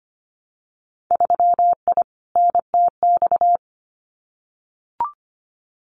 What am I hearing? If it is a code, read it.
Morse code, heard as a keyed tone, about 25 wpm, 700 Hz. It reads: 3I NTX
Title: Winter Field Day Exchanges 25wpm